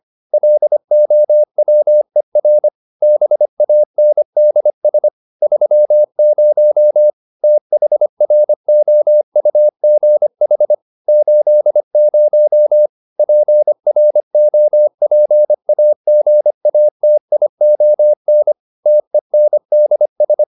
Morse code, heard as a keyed tone, about 25 wpm, 600 Hz. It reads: LOWER BANDS 30 THROUGH 80 PROPAGATION TENDS
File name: SS211 25